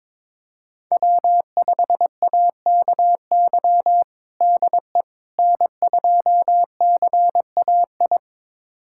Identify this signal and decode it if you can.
Morse code, heard as a keyed tone, about 22 wpm, 700 Hz. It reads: W5AKY DE N2CAI
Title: USA Callsign DE Callsign - 5 Characters - 22wpm